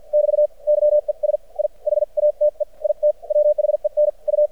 Morse code, heard as a keyed tone, about 56 wpm, spaced as for 45 wpm, 600 Hz. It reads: <BT> YES I HATE IT WHEN U
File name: archived / CRSnet